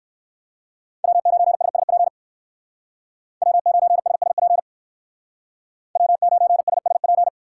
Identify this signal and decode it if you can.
Morse code, heard as a keyed tone, about 64 wpm, 700 Hz. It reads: W9SSP W9SSP W9SSP